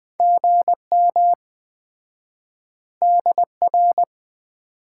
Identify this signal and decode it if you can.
Morse code, heard as a keyed tone, about 20 wpm, 700 Hz. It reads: GM DR
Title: Common QSO Phrases - 4x Word-Spacing - 20wpm